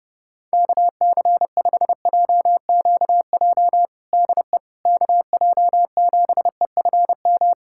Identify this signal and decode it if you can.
Morse code, heard as a keyed tone, about 30 wpm, 700 Hz. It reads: KC5JQJ DE KJ7EFM
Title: USA Callsign DE Callsign - 6 Characters - 30wpm